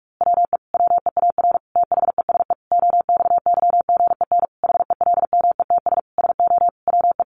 Morse code, heard as a keyed tone, about 44 wpm, 700 Hz. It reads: WE WEAR THESE OXYGEN HELMETS SO WE
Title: A Sound Of Thunder 45wpm